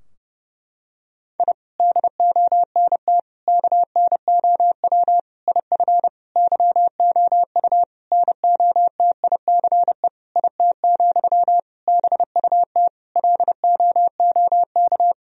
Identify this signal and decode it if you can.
Morse code, heard as a keyed tone, about 31 wpm, 700 Hz. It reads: I DONT KNOW IF YOU NOTICE IT, BUT LOOK
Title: Sentences from Top 1000 Words - Review Part 2 - 30wpm